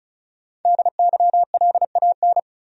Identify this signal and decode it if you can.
Morse code, heard as a keyed tone, about 34 wpm, 700 Hz. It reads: DYLAN
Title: USA Common Names - Max 5 Letters - 35wpm